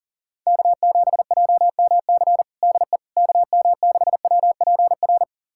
Morse code, heard as a keyed tone, about 40 wpm, 700 Hz. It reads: K7JMC DE KM6WPR
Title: USA Callsign DE Callsign 40wpm